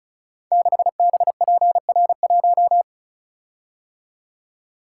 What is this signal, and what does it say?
Morse code, heard as a keyed tone, about 36 wpm, 700 Hz. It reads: 6BPR1
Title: Code Groups - 5 Letter:Numbers - 35wpm